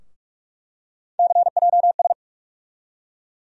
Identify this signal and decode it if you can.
Morse code, heard as a keyed tone, about 46 wpm, 700 Hz. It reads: CJS
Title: USA Call Sign Postfixes - 3 Letters 45wpm